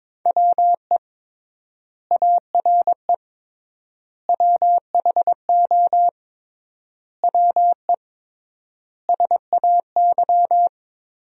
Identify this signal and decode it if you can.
Morse code, heard as a keyed tone, about 22 wpm, 700 Hz. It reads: WE ARE WHO WE SAY
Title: 3x Word-Spacing - Sentences from Top 200 Words - 22wpm